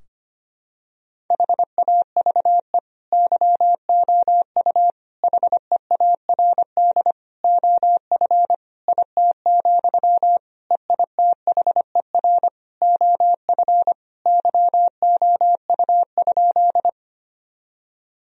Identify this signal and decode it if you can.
Morse code, heard as a keyed tone, about 25 wpm, 700 Hz. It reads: HAVE YOU HEARD OF IT, EITHER OF YOU?
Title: Sentences from Top 1000 Words - Review Part 4 - 25wpm